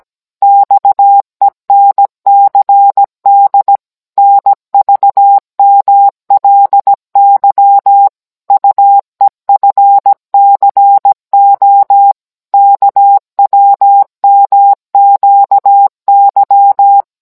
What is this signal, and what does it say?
Morse code, heard as a keyed tone, about 17 wpm, 800 Hz. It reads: XENCD NVMLY UEFCO KWMQY